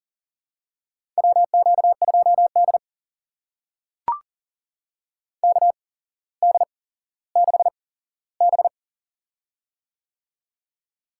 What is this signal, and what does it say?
Morse code, heard as a keyed tone, about 40 wpm, spaced as for 17 wpm, 700 Hz. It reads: WQ2D KD6B
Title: USA Call Signs - 4 Characters - 8x Character Spacing - 40wpm